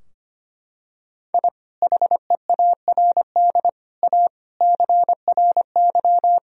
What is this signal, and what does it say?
Morse code, heard as a keyed tone, about 25 wpm, 700 Hz. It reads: I HEARD A CRY
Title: Sentences from Top 1000 Words - Review Part 2 - 25wpm